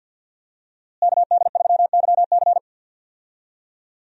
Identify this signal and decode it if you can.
Morse code, heard as a keyed tone, about 52 wpm, 700 Hz. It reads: KD3YC
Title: USA Call Signs - 5 Characters - Repeated 3x - 50wpm